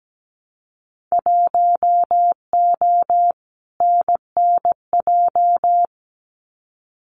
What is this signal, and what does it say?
Morse code, heard as a keyed tone, about 17 wpm, 700 Hz. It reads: 1O NNJ
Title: Winter Field Day Exchanges 17wpm